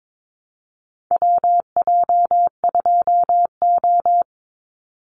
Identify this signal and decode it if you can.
Morse code, heard as a keyed tone, about 22 wpm, 700 Hz. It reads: WJ2O
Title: Call Signs Repeated 3x - 22wpm